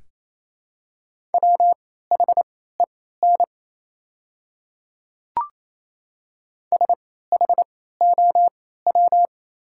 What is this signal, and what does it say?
Morse code, heard as a keyed tone, about 28 wpm, spaced as for 16 wpm, 700 Hz. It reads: WHEN SHOW
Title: 3x Character Spacing - Top 500 Words - 28wpm